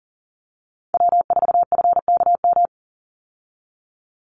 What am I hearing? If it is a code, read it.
Morse code, heard as a keyed tone, about 40 wpm, 700 Hz. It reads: W4FKM
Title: Call Signs Repeated 3x - 40wpm